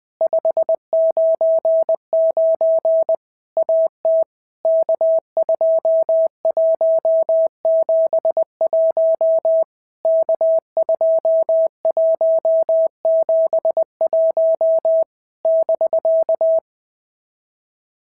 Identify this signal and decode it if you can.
Morse code, heard as a keyed tone, about 20 wpm, 650 Hz. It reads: 599 AT K2171 K2171 <BK>
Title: POTA QSO Practice - 20wpm